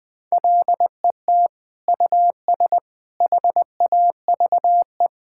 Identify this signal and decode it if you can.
Morse code, heard as a keyed tone, about 21 wpm, 700 Hz. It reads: LET US HAVE
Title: Sentences from Top 1000 Words - Review Part 6 - 20wpm